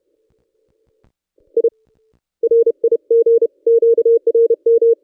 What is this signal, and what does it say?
Morse code, heard as a keyed tone, about 31 wpm, 460 Hz. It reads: I RIG QRM
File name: contests / ss-cw-k5tr-op-n5rz -1600